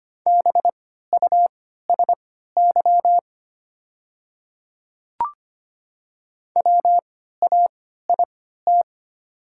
Morse code, heard as a keyed tone, about 25 wpm, spaced as for 14 wpm, 700 Hz. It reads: BUSY WAIT
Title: 3x Character Spacing - Top 500 Words - 25wpm